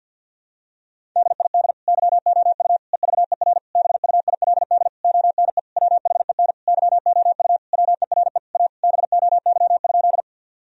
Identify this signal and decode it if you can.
Morse code, heard as a keyed tone, about 51 wpm, 700 Hz. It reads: DID YOU EVER BUILD ONE WHEN YOU WERE A BOY?
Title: Sentences from Top 1000 Words - Review Part 1 - 50wpm